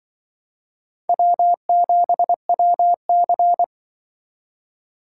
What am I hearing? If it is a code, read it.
Morse code, heard as a keyed tone, about 24 wpm, 700 Hz. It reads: W7WC
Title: Speed Racing - USA 1x2 Call Signs 20wpm